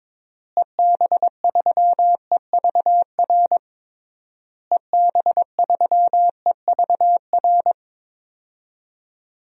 Morse code, heard as a keyed tone, about 22 wpm, 700 Hz. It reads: EB3EVR EB3EVR
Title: DXCC-100 Call Signs - 6 Characters - Repeated 3x - 22wpm